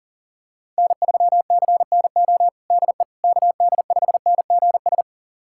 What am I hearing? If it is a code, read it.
Morse code, heard as a keyed tone, about 37 wpm, 700 Hz. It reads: N3CNO DE KD5NGS
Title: USA Callsign DE Callsign 40wpm